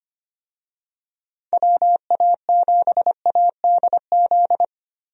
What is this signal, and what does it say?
Morse code, heard as a keyed tone, about 25 wpm, 700 Hz. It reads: WA7ADZ
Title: USA Call Signs - 6 Characters - Repeated 2x - 25wpm